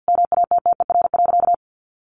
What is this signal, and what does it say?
Morse code, heard as a keyed tone, about 50 wpm, 700 Hz. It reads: MATTER.
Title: Sentences from Top 700 Words 50wpm